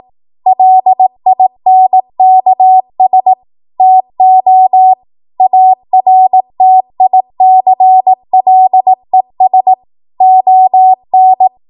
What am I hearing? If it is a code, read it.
Morse code, heard as a keyed tone, about 18 wpm, 750 Hz. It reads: LINKS TO ARTICLES ON
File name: Archive / 18 WPM / 18WPM